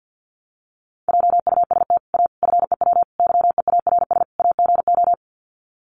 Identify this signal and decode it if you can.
Morse code, heard as a keyed tone, about 50 wpm, 700 Hz. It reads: JUST A FEW YEARS AGO
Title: Sets of 5 Words Encore 50wpm